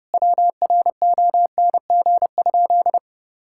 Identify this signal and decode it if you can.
Morse code, heard as a keyed tone, about 30 wpm, 700 Hz. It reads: WRONG?
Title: Sentences from Top 1000 Words - Review Part 6 - 30wpm